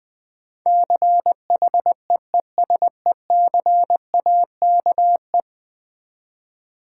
Morse code, heard as a keyed tone, about 20 wpm, 700 Hz. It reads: CHEESECAKE
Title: Difficult Words with Hints - 20wpm